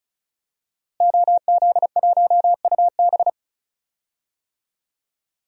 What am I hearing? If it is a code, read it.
Morse code, heard as a keyed tone, about 34 wpm, 700 Hz. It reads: OZ1UB
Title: DXCC-100 Call Signs - 5 Characters - Repeated 2x - 35wpm